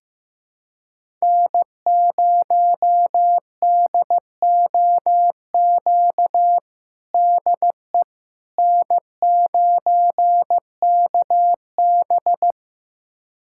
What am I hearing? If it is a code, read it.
Morse code, heard as a keyed tone, about 15 wpm, 700 Hz. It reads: N0DOQ DE N9KB
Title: USA Callsign DE Callsign 15wpm